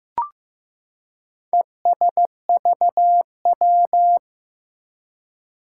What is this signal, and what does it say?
Morse code, heard as a keyed tone, about 15 wpm, 700 Hz. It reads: ESVW
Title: Four Challenging Letter-Number Combinations 15wpm